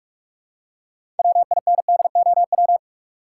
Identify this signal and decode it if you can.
Morse code, heard as a keyed tone, about 41 wpm, 700 Hz. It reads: WINDOW
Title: Six Letter Words - 45wpm